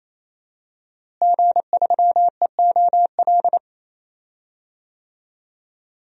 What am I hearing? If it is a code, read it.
Morse code, heard as a keyed tone, about 27 wpm, 700 Hz. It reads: G3EOL
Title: DXCC-100 Call Signs - 5 Characters - Repeated 3x - 28wpm